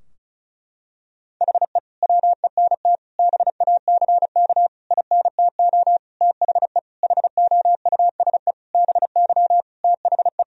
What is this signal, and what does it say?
Morse code, heard as a keyed tone, about 36 wpm, 700 Hz. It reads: HE WENT BACK INTO THE HOUSE BY THE
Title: Sentences from Top 1000 Words - Review Part 2 - 35wpm